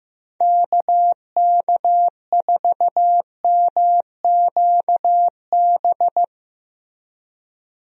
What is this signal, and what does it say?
Morse code, heard as a keyed tone, about 15 wpm, 700 Hz. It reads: KK4MQB
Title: USA Call Signs - 6 Characters - Repeated 2x - 15wpm